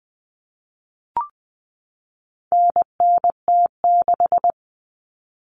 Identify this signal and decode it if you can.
Morse code, heard as a keyed tone, about 20 wpm, 700 Hz. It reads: NNT6
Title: Common Cut Numbers - 4-Digit 20wpm